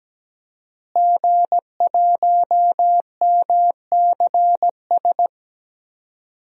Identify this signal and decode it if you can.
Morse code, heard as a keyed tone, about 17 wpm, 700 Hz. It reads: G1MCS
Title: DXCC-100 Call Signs - 5 Characters - Repeated 3x - 17wpm